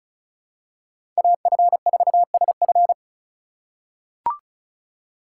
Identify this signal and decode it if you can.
Morse code, heard as a keyed tone, about 34 wpm, 700 Hz. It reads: AF4SF
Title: USA Call Signs - 5 Characters - 4x Character Spacing - 35wpm